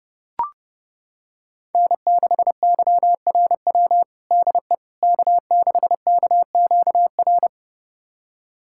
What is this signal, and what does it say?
Morse code, heard as a keyed tone, about 30 wpm, 700 Hz. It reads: N6YRW DE K6KQR
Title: USA Callsign DE Callsign - 5 Characters - 30wpm